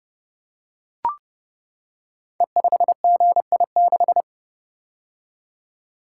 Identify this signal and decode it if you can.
Morse code, heard as a keyed tone, about 30 wpm, 700 Hz. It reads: E5GI6
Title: Code Groups - 5 Letter:Numbers - 30wpm